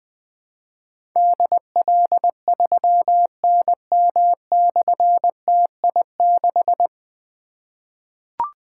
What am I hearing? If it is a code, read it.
Morse code, heard as a keyed tone, about 21 wpm, 700 Hz. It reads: DL3NM/TI6
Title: Difficult Call Signs 20wpm